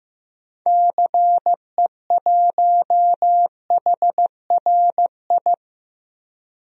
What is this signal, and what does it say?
Morse code, heard as a keyed tone, about 15 wpm, 700 Hz. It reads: CE1HRI
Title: DXCC-100 Call Signs - 6 Characters - Repeated 2x - 15wpm